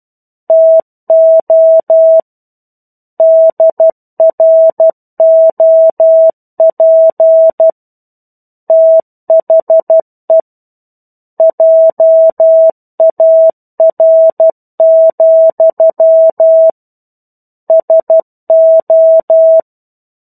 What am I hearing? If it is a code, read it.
Morse code, heard as a keyed tone, about 12 wpm, 650 Hz. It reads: TO DROP THE JAR, SO
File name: 01-Chapter-01 12 WPM